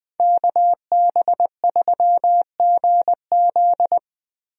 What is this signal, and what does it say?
Morse code, heard as a keyed tone, about 21 wpm, 700 Hz. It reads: KB3GZ
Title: USA Call Signs - 5 Characters - Repeated 2x - 20wpm